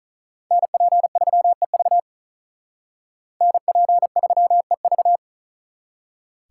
Morse code, heard as a keyed tone, about 43 wpm, 700 Hz. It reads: NP3EV NP3EV
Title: Speed Racing - USA 2x2 Call Signs 35wpm